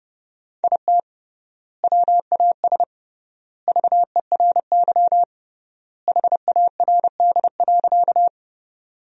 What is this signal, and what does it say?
Morse code, heard as a keyed tone, about 30 wpm, 700 Hz. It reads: IT WAS VERY HARD.
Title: 3x Word-Spacing - Sentences from Top 300 Words - 30wpm